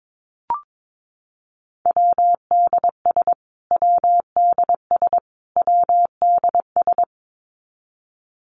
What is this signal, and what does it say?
Morse code, heard as a keyed tone, about 22 wpm, 700 Hz. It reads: WDS WDS WDS
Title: Abbreviations Repeated 3x - 22wpm